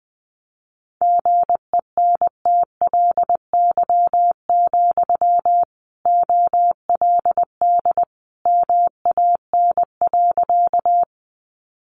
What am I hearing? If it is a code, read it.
Morse code, heard as a keyed tone, about 20 wpm, 700 Hz. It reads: GENTLY, OLD MAN.
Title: Sentences from Top 1500 Words 20wpm